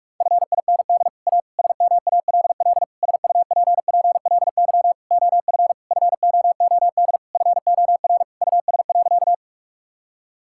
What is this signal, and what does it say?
Morse code, heard as a keyed tone, about 46 wpm, 700 Hz. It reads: FIND A SMALL SUPPLY OF FOOD FOR US.
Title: Sentences from Top 1000 Words - Review Part 5 - 45wpm